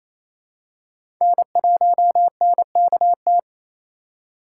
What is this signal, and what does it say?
Morse code, heard as a keyed tone, about 28 wpm, 700 Hz. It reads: N1NKT
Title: USA Call Signs - 5 Characters - 2x Character Spacing - 28wpm